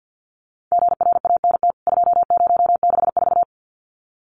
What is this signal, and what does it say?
Morse code, heard as a keyed tone, about 49 wpm, 700 Hz. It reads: GRANT 2064
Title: CWT Exchange 50wpm